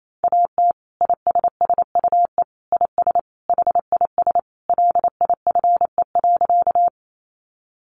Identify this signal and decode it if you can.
Morse code, heard as a keyed tone, about 28 wpm, 700 Hz. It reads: AT ISSUE IS HIS LIFE.
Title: Sentences from Top 900 Words 28wpm